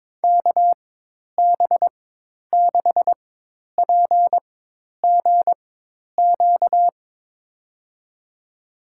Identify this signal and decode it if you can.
Morse code, heard as a keyed tone, about 22 wpm, spaced as for 10 wpm, 700 Hz. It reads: KB6PGQ
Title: USA Call Signs - 6 Characters - 4x Character Spacing - 22wpm